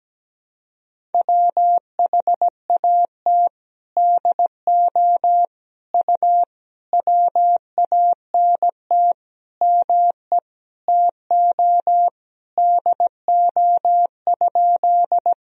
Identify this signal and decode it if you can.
Morse code, heard as a keyed tone, about 17 wpm, 700 Hz. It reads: WHAT DO U WANT ME TO DO?
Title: Sentences from Top 200 Words and Abbreviations 17wpm